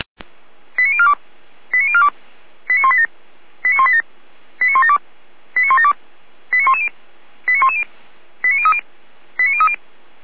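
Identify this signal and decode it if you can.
Five-tone selective call (ZVEI), heard as quick runs of five tones, tones of about 70 ms. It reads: 89042 89042 89187 89187 89172 89172 89110 89110 89030 89030
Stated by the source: ZVEI five-tone selective call, received by SDR.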